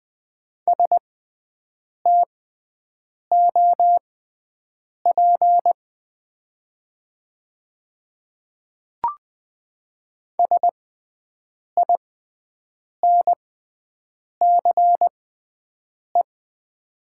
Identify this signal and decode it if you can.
Morse code, heard as a keyed tone, about 20 wpm, spaced as for 7 wpm, 700 Hz. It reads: STOP SINCE
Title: 6x Character Spacing - Top 300 Words - 20wpm